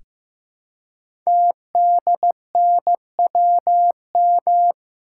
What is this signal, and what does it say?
Morse code, heard as a keyed tone, about 15 wpm, 700 Hz. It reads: TDNWM
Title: Code Groups - 5 Letters - 15wpm